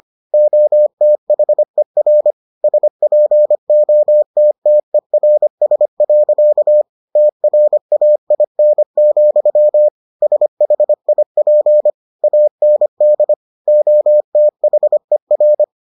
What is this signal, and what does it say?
Morse code, heard as a keyed tone, about 25 wpm, 600 Hz. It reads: OTHER SPOTTERS. TRAIN, SHIP AND OTHER